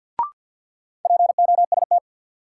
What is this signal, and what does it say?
Morse code, heard as a keyed tone, about 51 wpm, 700 Hz. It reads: POST
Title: Sentences from Top 1000 Words - New Word Drill - 50wpm